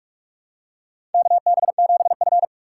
Morse code, heard as a keyed tone, about 44 wpm, 700 Hz. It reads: KB7F
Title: USA Call Signs - 4 Characters - 3x Character Spacing - 45wpm